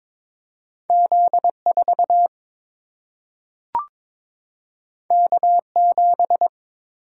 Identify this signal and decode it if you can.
Morse code, heard as a keyed tone, about 22 wpm, 700 Hz. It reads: Z4 K7
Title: Single-Letter Single-Number 22wpm